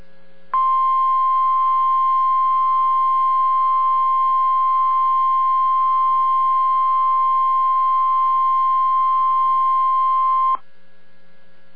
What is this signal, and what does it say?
Emergency Alert System (SAME) tones (the 1050 Hz attention tone). It is heard from 0.5 to 10.6 s.